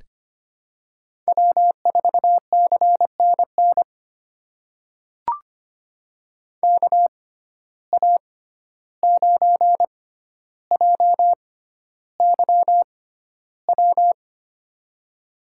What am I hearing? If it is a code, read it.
Morse code, heard as a keyed tone, about 25 wpm, spaced as for 12 wpm, 700 Hz. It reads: W4CNN KA9JYW
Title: USA Call Signs - 6x Character Spacing - 25wpm